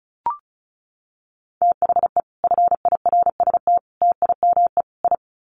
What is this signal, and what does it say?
Morse code, heard as a keyed tone, about 34 wpm, 700 Hz. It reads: THE FIRST TIME I
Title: Sets of 4 Words 35wpm